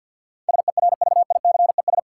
Morse code, heard as a keyed tone, about 53 wpm, 700 Hz. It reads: SERVICES
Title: Sets of 5 Words - New Word Drill - 50wpm